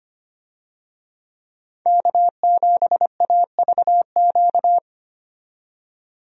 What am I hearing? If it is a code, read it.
Morse code, heard as a keyed tone, about 25 wpm, 700 Hz. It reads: K7AVQ
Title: USA Call Signs - 5 Characters - 25wpm